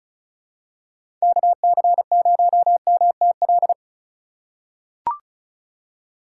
Morse code, heard as a keyed tone, about 34 wpm, 700 Hz. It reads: KC0MTL
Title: USA Call Signs - 6 Characters - 3x Character Spacing - 35wpm